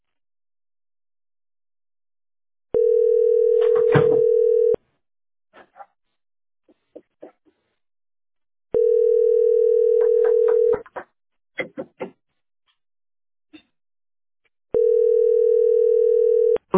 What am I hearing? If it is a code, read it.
Ringback tone (North America), 440 + 480 Hz.